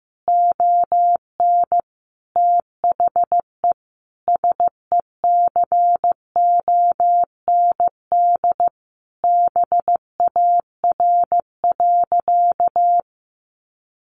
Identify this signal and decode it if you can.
Morse code, heard as a keyed tone, about 15 wpm, 700 Hz. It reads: ON THE SECOND BAR.
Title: Easy Phrases 15wpm